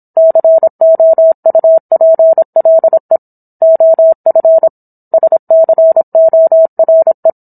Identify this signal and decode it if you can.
Morse code, heard as a keyed tone, about 27 wpm, 650 Hz. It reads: COUPLE OF SCORE